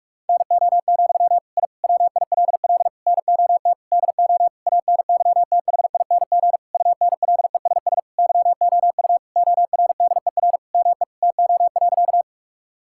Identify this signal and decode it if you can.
Morse code, heard as a keyed tone, about 46 wpm, 700 Hz. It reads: NO, I WILL NOT DO ANYTHING UNLESS YOU ORDER ME TO.
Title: Sentences from Top 1000 Words - Review Part 6 - 45wpm